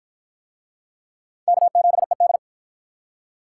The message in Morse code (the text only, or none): K7ED